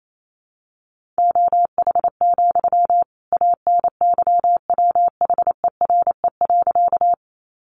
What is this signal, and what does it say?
Morse code, heard as a keyed tone, about 28 wpm, 700 Hz. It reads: OH, ANYWHERE.
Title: Sentences from Top 1400 Words 28wpm